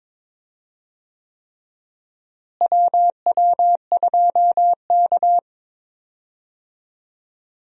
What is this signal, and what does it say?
Morse code, heard as a keyed tone, about 22 wpm, 700 Hz. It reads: WW2K
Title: USA Call Signs - 4 Characters - Repeated 3x - 22wpm